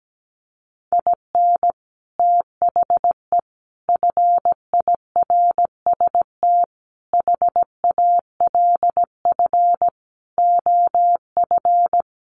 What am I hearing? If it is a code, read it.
Morse code, heard as a keyed tone, about 17 wpm, 700 Hz. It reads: IN THE FIRST HALF OF
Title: Sets of 5 Words 17wpm